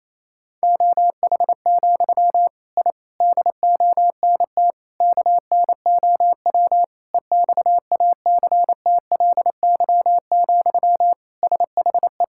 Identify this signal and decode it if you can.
Morse code, heard as a keyed tone, about 28 wpm, 700 Hz. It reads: OH, I DONT KNOW EXACTLY, SHE
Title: Sentences from Top 1000 Words - Review Part 6 - 28wpm